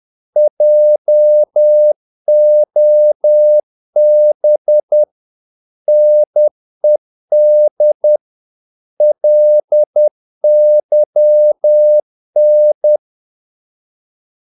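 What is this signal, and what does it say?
Morse code, heard as a keyed tone, about 10 wpm, 600 Hz. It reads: JOB NED LYN
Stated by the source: Morse practice of short Danish words at 10 wpm, from CW med Gnister.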